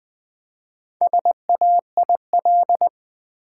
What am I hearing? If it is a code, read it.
Morse code, heard as a keyed tone, about 20 wpm, 700 Hz. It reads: SAIL
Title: Four Letter Words - 20wpm